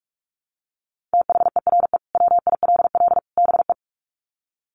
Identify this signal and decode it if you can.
Morse code, heard as a keyed tone, about 44 wpm, 700 Hz. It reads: THERE WILL BE
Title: Sets of 3 Words 45wpm